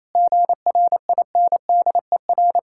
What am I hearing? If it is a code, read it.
Morse code, heard as a keyed tone, about 28 wpm, 700 Hz. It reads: GRINDER